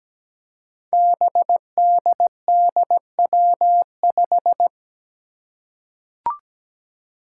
Morse code, heard as a keyed tone, about 17 wpm, 700 Hz. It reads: BDDW5
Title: Five Challenging Letter-Number Combinations 17wpm